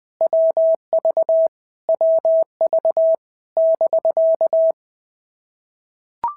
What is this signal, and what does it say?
Morse code, heard as a keyed tone, about 20 wpm, 650 Hz. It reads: WV WV <BK>
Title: POTA QSO Practice - 20wpm